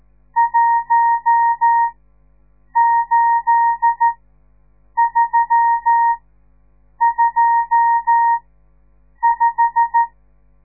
Morse code, heard as a keyed tone, about 13 wpm, spaced as for 7 wpm, 930 Hz. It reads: 18325
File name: radiomorningstar / morse / les 10